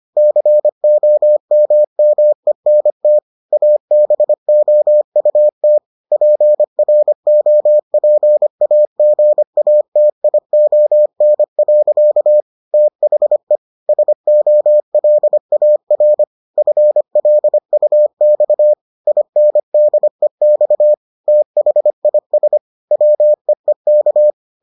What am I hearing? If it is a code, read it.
Morse code, heard as a keyed tone, about 25 wpm, 600 Hz. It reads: COMMENT ABOUT PROPAGATION. THE SOLAR FLUX INDEX THIS WEEK